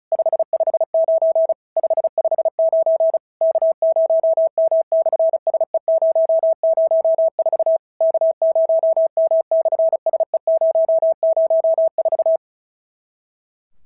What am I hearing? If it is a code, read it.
Morse code, heard as a keyed tone, about 36 wpm, 650 Hz. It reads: 559 559 K0M/SE004 K0M/SE004
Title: SOTA QSO Practice 35wpm